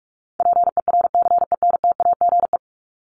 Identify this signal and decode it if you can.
Morse code, heard as a keyed tone, about 44 wpm, 700 Hz. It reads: PERCENTAGE
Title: Top 1001-1250 Words 45wpm